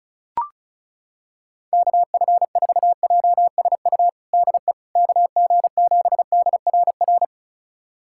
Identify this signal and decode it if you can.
Morse code, heard as a keyed tone, about 34 wpm, 700 Hz. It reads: KF4JSU DE KG7DRR
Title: USA Callsign DE Callsign 35wpm